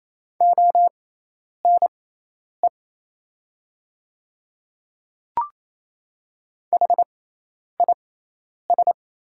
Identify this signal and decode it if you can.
Morse code, heard as a keyed tone, about 28 wpm, spaced as for 10 wpm, 700 Hz. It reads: ONE HIS
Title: 6x Character Spacing - Top 100 Words - 28wpm